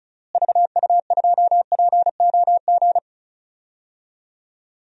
Morse code, heard as a keyed tone, about 36 wpm, 700 Hz. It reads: VU2POG
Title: DXCC-100 Call Signs - 6 Characters - Repeated 3x - 35wpm